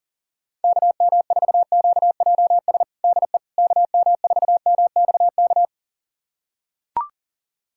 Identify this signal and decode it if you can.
Morse code, heard as a keyed tone, about 40 wpm, 700 Hz. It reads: KM4QJS DE KM4MXK
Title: USA Callsign DE Callsign - 6 Characters - 40wpm